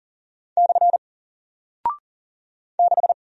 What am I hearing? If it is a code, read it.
Morse code, heard as a keyed tone, about 40 wpm, 700 Hz. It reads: / 6
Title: Single Character (Letter:Number:Punctuation:Prosign) Alt 40wpm